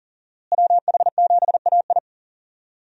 Morse code, heard as a keyed tone, about 40 wpm, 700 Hz. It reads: WH7AI
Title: USA Call Signs - 5 Characters - Repeated 2x - 40wpm